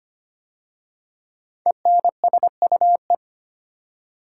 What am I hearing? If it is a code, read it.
Morse code, heard as a keyed tone, about 25 wpm, 700 Hz. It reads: ENSUE
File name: 5 Letter Words - First 3 Letters Given - 25wpm